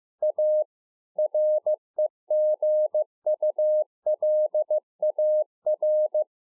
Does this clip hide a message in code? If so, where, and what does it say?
Yes: Morse code from 0.2 to 6.2 s, heard as a keyed tone, about 15 wpm, 610 Hz. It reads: A REGULAR